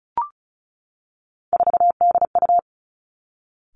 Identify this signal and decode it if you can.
Morse code, heard as a keyed tone, about 34 wpm, 700 Hz. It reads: 4DU
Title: Three Challenging Letter-Number Combinations 35wpm